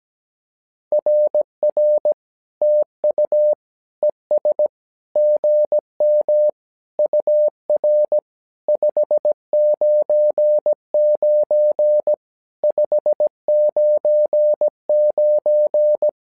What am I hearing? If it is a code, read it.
Morse code, heard as a keyed tone, about 17 wpm, 600 Hz. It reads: RR TU ES GM UR 599 599
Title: SOTA QSO Practice 17wpm